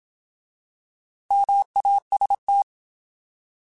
Morse code, heard as a keyed tone, about 27 wpm, 800 Hz. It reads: MAST